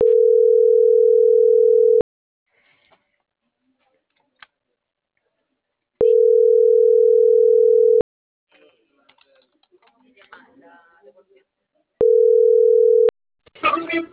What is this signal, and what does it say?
Ringback tone (North America), 440 + 480 Hz.